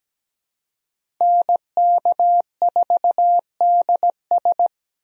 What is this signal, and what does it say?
Morse code, heard as a keyed tone, about 17 wpm, 700 Hz. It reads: NK4DS
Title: USA Call Signs - 5 Characters - Repeated 2x - 17wpm